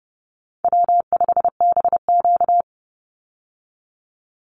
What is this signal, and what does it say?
Morse code, heard as a keyed tone, about 30 wpm, 700 Hz. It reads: W5BQ
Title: Call Signs Repeated 3x - 30wpm